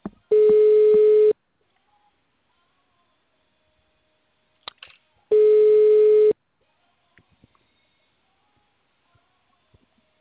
Ringback tone (Europe, 425 Hz).
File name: out- -2003- - - .191